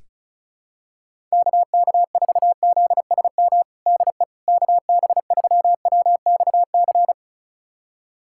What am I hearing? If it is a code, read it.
Morse code, heard as a keyed tone, about 34 wpm, 700 Hz. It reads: KK4ZSM DE KB3WXC
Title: USA Callsign DE Callsign - 6 Characters - 35wpm